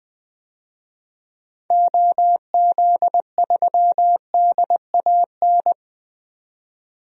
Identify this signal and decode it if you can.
Morse code, heard as a keyed tone, about 20 wpm, 700 Hz. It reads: OZ3DAN
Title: DXCC-100 Call Signs - 6 Characters - 20wpm